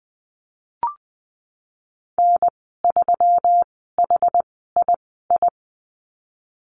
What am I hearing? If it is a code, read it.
Morse code, heard as a keyed tone, about 20 wpm, spaced as for 15 wpm, 700 Hz. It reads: N3HII
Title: 161 Lesson 26g - Letter g - Call Signs 20wpm